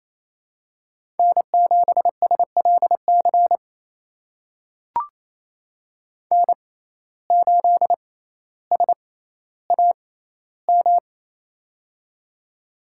Morse code, heard as a keyed tone, about 28 wpm, spaced as for 14 wpm, 700 Hz. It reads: N7SLC N8SAM
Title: USA Call Signs - 5 Characters - 6x Character Spacing - 28wpm